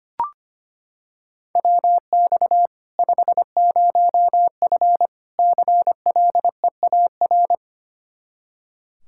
Morse code, heard as a keyed tone, about 25 wpm, 700 Hz. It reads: WX 50F CLEAR
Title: K1USN SST QSO Practice 25wpm